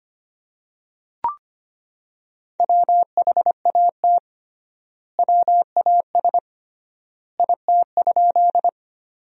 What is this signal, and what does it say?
Morse code, heard as a keyed tone, about 25 wpm, 700 Hz. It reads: WHAT WAS IT?
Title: 3x Word-Spacing - Sentences from Top 100 Words - 25wpm